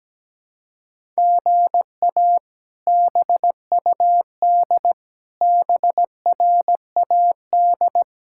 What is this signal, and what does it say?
Morse code, heard as a keyed tone, about 17 wpm, 700 Hz. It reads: GA BUD BRAD
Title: K1USN SST QSO Practice - 17wpm